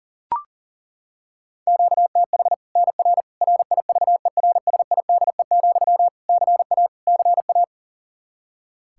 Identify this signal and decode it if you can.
Morse code, heard as a keyed tone, about 40 wpm, 700 Hz. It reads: QTH NR RIVERSIDE, CA CA
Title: Practice QSO - Exchange 1 - Beginning - 40wpm